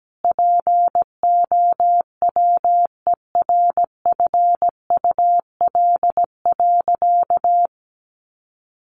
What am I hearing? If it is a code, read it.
Morse code, heard as a keyed tone, about 17 wpm, 700 Hz. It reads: POWERFUL.
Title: Sentences from Top 1300 Words 17wpm